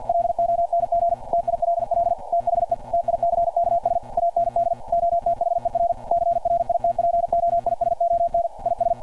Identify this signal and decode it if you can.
Morse code, heard as a keyed tone, about 50 wpm, 690 Hz. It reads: LONG TILL THE TICKS AND OTHER CRIT POISON IS
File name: archived / CRSnet